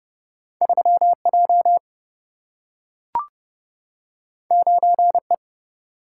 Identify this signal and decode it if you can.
Morse code, heard as a keyed tone, about 31 wpm, 700 Hz. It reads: 3J 9E
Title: Single-Letter Single-Number 30wpm